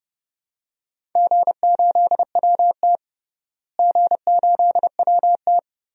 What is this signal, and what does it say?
Morse code, heard as a keyed tone, about 32 wpm, 700 Hz. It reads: G8WT G8WT
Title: DXCC-100 Call Signs - 4 Characters - Repeated 2x - 30wpm